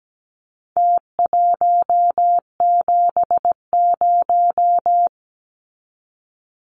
Morse code, heard as a keyed tone, about 17 wpm, 700 Hz. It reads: T170
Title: Numbers - Contest Style - 4-Digit 17wpm